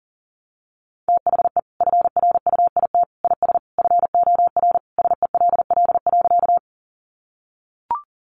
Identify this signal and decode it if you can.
Morse code, heard as a keyed tone, about 40 wpm, 700 Hz. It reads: THE FRUIT IS FOR SELL.
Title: Sentences from Top 1300 Words 40wpm